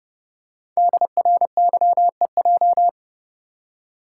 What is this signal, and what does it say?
Morse code, heard as a keyed tone, about 30 wpm, 700 Hz. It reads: DRYEJ
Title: Code Groups - 5 Letters - 30wpm